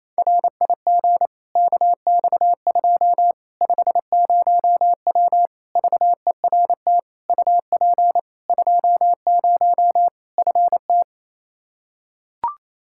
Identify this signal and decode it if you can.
Morse code, heard as a keyed tone, about 28 wpm, 700 Hz. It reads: RIG KX2 50W VERT UP 20 FT
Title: Practice QSO - Exchange 2 - Info - 28wpm